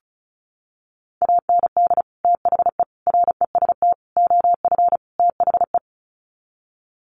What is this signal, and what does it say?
Morse code, heard as a keyed tone, about 34 wpm, 700 Hz. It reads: AND THE REST OF THE
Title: Sets of 5 Words 35wpm